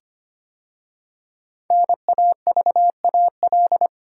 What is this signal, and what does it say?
Morse code, heard as a keyed tone, about 25 wpm, 700 Hz. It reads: NAVAL
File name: 5 Letter Words - First 3 Letters Given - 25wpm